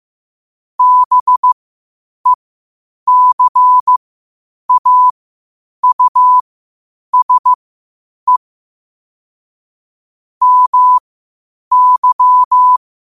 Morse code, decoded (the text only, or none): BECAUSE MY